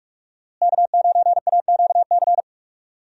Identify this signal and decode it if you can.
Morse code, heard as a keyed tone, about 46 wpm, 700 Hz. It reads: K9AQC
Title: USA Call Signs - 5 Characters - Repeated 2x - 45wpm